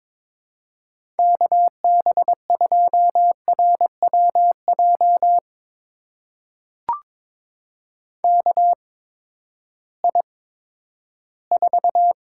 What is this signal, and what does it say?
Morse code, heard as a keyed tone, about 22 wpm, spaced as for 13 wpm, 700 Hz. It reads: KB2RWJ KI4
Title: USA Call Signs - 6 Characters - 8x Character Spacing - 22wpm